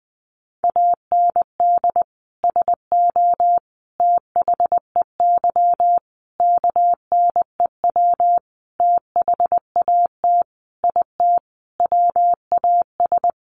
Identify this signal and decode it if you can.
Morse code, heard as a keyed tone, about 20 wpm, 700 Hz. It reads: AND SO THEY KNEW THAT IT WAS
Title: Sentences from Top 400 Words 20wpm